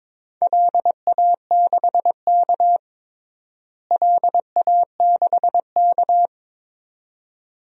Morse code, heard as a keyed tone, about 22 wpm, 700 Hz. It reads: LA6K LA6K
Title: DXCC-100 Call Signs - 4 Characters - Repeated 2x - 22wpm